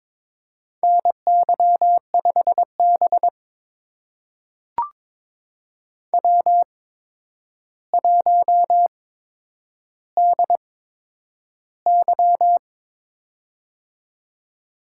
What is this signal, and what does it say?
Morse code, heard as a keyed tone, about 23 wpm, spaced as for 10 wpm, 700 Hz. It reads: NY5B W1DY
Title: USA Call Signs - 4 Characters - 8x Character Spacing - 22wpm